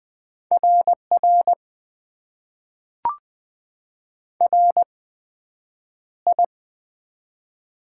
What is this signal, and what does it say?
Morse code, heard as a keyed tone, about 20 wpm, spaced as for 9 wpm, 700 Hz. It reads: RR RI
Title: Top 90 QSO Elements - 8x Character Spacing - 20wpm